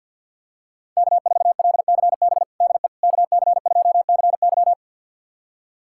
Morse code, heard as a keyed tone, about 50 wpm, 700 Hz. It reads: K4LCB DE KC2CY
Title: USA Callsign DE Callsign - 5 Characters - 50wpm